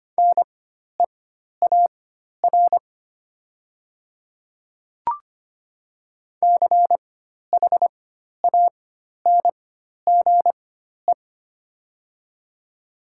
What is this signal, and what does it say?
Morse code, heard as a keyed tone, about 25 wpm, spaced as for 12 wpm, 700 Hz. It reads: NEAR CHANGE
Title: 4x Character Spacing - Top 200 Words - 25wpm